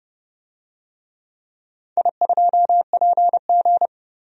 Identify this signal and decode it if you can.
Morse code, heard as a keyed tone, about 30 wpm, 700 Hz. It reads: I2PG
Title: DXCC-100 Call Signs - 4 Characters - 30wpm